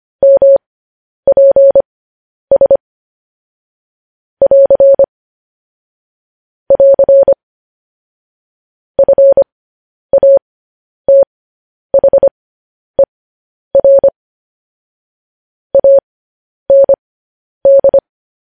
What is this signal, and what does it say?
Morse code, heard as a keyed tone, about 25 wpm, spaced as for 10 wpm, 550 Hz. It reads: MPS <AR> <AR> FATHER AND
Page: news headlines in morse code at 10 wpm